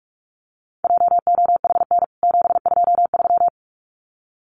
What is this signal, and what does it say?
Morse code, heard as a keyed tone, about 44 wpm, 700 Hz. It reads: JOHN 723
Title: CWT Exchange 45wpm